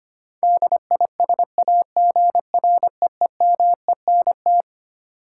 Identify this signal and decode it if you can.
Morse code, heard as a keyed tone, about 25 wpm, 700 Hz. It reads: DISAGREEMENT
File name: Twelve Letter Words - 25wpm